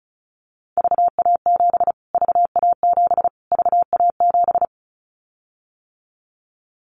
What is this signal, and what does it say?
Morse code, heard as a keyed tone, about 34 wpm, 700 Hz. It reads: VA7 VA7 VA7
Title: Callsign Prefixes for Top 25 DXCC - Repeated 3x - 35wpm